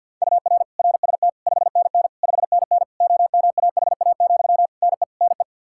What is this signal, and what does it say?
Morse code, heard as a keyed tone, about 46 wpm, 700 Hz. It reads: UR RST 5NN 5NN OMAHA, NE NE
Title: Practice QSO - Exchange 1 - Beginning - 50wpm